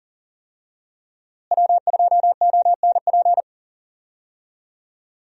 Morse code, heard as a keyed tone, about 40 wpm, 700 Hz. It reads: W2ONP
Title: USA Call Signs - 5 Characters - Repeated 3x - 40wpm